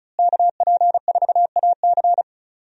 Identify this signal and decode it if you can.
Morse code, heard as a keyed tone, about 34 wpm, 700 Hz. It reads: KP4AC
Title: USA Call Signs - 5 Characters - Repeated 3x - 35wpm